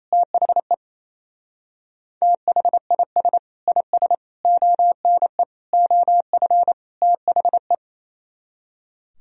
Morse code, heard as a keyed tone, about 34 wpm, 700 Hz. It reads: THE THIS IS ONE OF THE
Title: Speed-Racing - Sets of 5 Words 28wpm